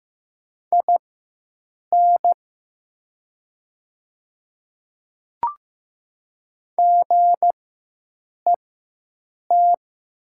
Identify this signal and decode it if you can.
Morse code, heard as a keyed tone, about 15 wpm, spaced as for 7 wpm, 690 Hz. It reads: IN GET
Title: 4x Character Spacing - Top 200 Words - 15wpm